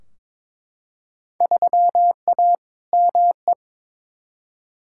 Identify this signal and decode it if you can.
Morse code, heard as a keyed tone, about 22 wpm, 700 Hz. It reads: 3A ME
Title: Field Day Exchanges - 22wpm